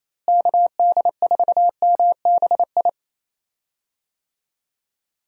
Morse code, heard as a keyed tone, about 28 wpm, 700 Hz. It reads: KD4MBI
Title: USA Call Signs - 6 Characters - Repeated 2x - 28wpm